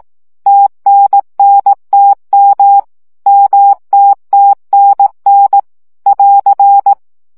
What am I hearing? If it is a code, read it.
Morse code, heard as a keyed tone, about 18 wpm, 800 Hz. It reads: TNNTM MTTNN <AR>
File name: CW gyakorlas t n m 15 baud